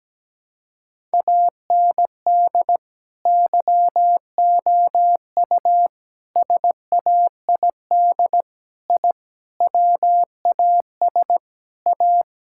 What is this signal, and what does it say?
Morse code, heard as a keyed tone, about 17 wpm, 700 Hz. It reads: AND YOU SAID I WAS A
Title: Sentences from Top 1000 Words - Review Part 5 - 17wpm